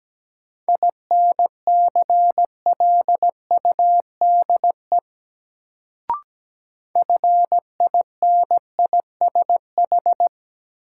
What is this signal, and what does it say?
Morse code, heard as a keyed tone, about 17 wpm, 700 Hz. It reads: INCLUDE FINISH
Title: Sentences from Top 1000 Words - Review Part 3 - New Word Drill - 17wpm